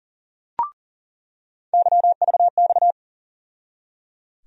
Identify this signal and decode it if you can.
Morse code, heard as a keyed tone, about 37 wpm, 700 Hz. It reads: YVX
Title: USA Call Sign Postfixes - 3 Letters 40wpm